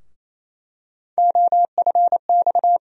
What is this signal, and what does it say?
Morse code, heard as a keyed tone, about 27 wpm, 700 Hz. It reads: OFX
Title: USA Call Sign Postfixes - 3 Letters 28wpm